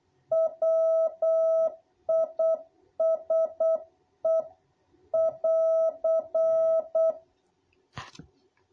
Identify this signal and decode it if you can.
Morse code, heard as a keyed tone, about 8 wpm, 660 Hz. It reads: WISE<AR>
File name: jh8cbh / 2021cw